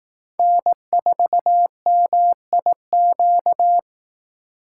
Morse code, heard as a keyed tone, about 18 wpm, 700 Hz. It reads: N4MIQ
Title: Speed Racing - USA 1x3 Call Signs 15wpm